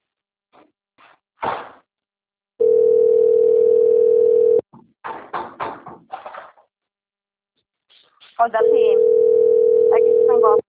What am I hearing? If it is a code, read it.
Ringback tone (North America), 440 + 480 Hz, with speech.
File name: rg-1600-unknown- - - .36976